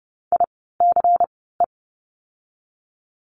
Morse code, heard as a keyed tone, about 30 wpm, spaced as for 17 wpm, 700 Hz. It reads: ICE
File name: Lesson 16f - Letter c - Words 30wpm